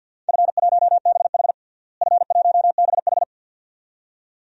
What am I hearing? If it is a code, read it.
Morse code, heard as a keyed tone, about 52 wpm, 700 Hz. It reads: F1BH F1BH
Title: DXCC-100 Call Signs - 4 Characters - Repeated 2x - 50wpm